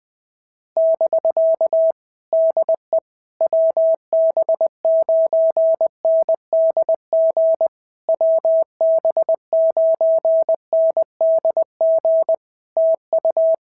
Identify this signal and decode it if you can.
Morse code, heard as a keyed tone, about 20 wpm, 650 Hz. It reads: <BK> DE WB9NDG WB9NDG TU
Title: POTA QSO Practice - 20wpm